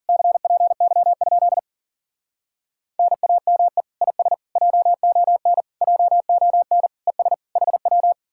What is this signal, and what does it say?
Morse code, heard as a keyed tone, about 52 wpm, 700 Hz. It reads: CPY? NAME IS JON JON ES HW